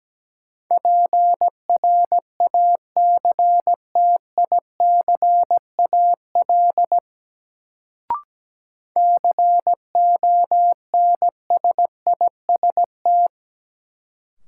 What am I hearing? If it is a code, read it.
Morse code, heard as a keyed tone, about 17 wpm, 700 Hz. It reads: PRACTICAL CONSIST
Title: Sentences from Top 1400 Words - New Word Drill Alt - 17wpm